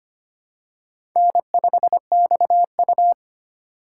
Morse code, heard as a keyed tone, about 25 wpm, 700 Hz. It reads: N5XU
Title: USA Call Signs - 4 Characters - 3x Character Spacing - 25wpm